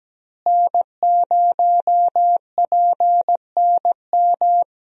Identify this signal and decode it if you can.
Morse code, heard as a keyed tone, about 17 wpm, 700 Hz. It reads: N0PNM
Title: USA Call Signs - 5 Characters - Repeated 2x - 17wpm